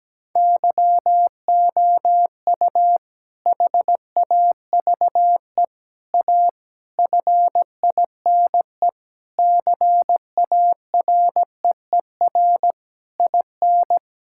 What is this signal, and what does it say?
Morse code, heard as a keyed tone, about 17 wpm, 700 Hz. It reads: YOU HAVE A FINE CAREER IN
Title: Sentences from Top 1000 Words - Review Part 6 - 17wpm